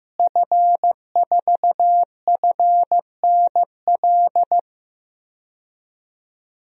Morse code, heard as a keyed tone, about 15 wpm, 700 Hz. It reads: F4FNL
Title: DXCC-100 Call Signs - 5 Characters - Repeated 3x - 15wpm